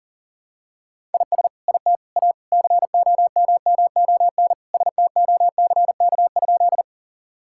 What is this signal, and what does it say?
Morse code, heard as a keyed tone, about 37 wpm, 700 Hz. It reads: IS IT A COMMON STOCK?
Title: Sentences from Top 1000 Words - Review Part 5 - 40wpm